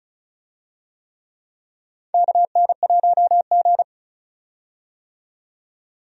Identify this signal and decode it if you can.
Morse code, heard as a keyed tone, about 36 wpm, 700 Hz. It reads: KN1G
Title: USA Call Signs - 4 Characters - 35wpm